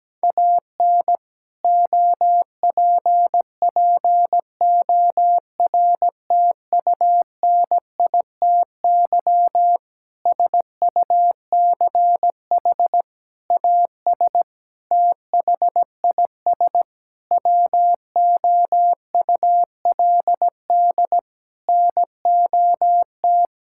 Morse code, heard as a keyed tone, about 17 wpm, 700 Hz. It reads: AN OPPORTUNITY SUCH AS THIS WOULD NOT
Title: Sentences from Top 1600 Words 17wpm